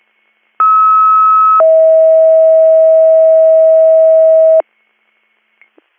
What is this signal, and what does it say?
Two-tone page, heard as a one-second tone, then a three-second tone, about 1290 then 640 Hz. It is heard from 0.6 to 4.6 s.